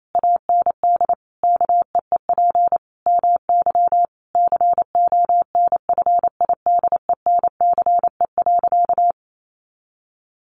Morse code, heard as a keyed tone, about 28 wpm, 700 Hz. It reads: AND KEEP MY CONFIDENCE.
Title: Sentences from Top 1500 Words 28wpm